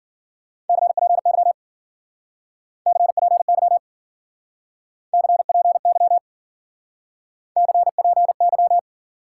Morse code, heard as a keyed tone, about 49 wpm, 700 Hz. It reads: CPY CPY CPY CPY
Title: Speed-Racing Top 90 QSO Elements 40wpm